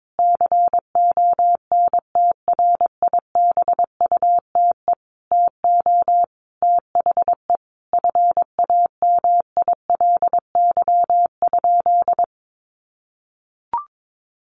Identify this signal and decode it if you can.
Morse code, heard as a keyed tone, about 22 wpm, 700 Hz. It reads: CONTRIBUTE TO THE FAMILY?
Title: Sentences from Top 1400 Words 22wpm